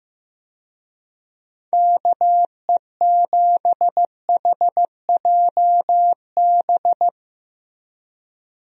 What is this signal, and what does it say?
Morse code, heard as a keyed tone, about 15 wpm, 700 Hz. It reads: KE7HJB
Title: USA Call Signs - 6 Characters - Repeated 2x - 15wpm